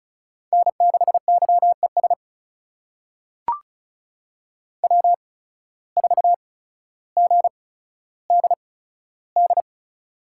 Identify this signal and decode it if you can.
Morse code, heard as a keyed tone, about 34 wpm, spaced as for 15 wpm, 690 Hz. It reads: N6YES W4GDD
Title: USA Call Signs - 5 Characters - 8x Character Spacing - 35wpm